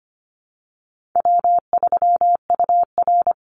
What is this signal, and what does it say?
Morse code, heard as a keyed tone, about 25 wpm, 700 Hz. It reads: W3UR
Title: Easy Call Signs - 25wpm